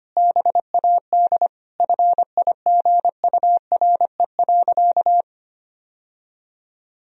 Morse code, heard as a keyed tone, about 25 wpm, 700 Hz. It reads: BAD FIGURE.
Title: Sentences from Top 1000 Words - Review Part 2 - 25wpm